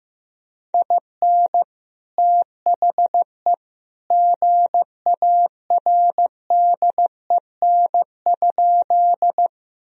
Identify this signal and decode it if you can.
Morse code, heard as a keyed tone, about 15 wpm, 700 Hz. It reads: IN THE GARDEN?
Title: Sentences from Top 1000 Words - Review Part 4 - 15wpm